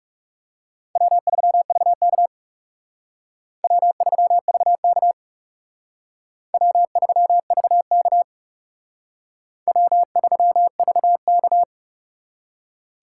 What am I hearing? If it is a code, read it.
Morse code, heard as a keyed tone, about 38 wpm, 700 Hz. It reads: W3VK W3VK W3VK W3VK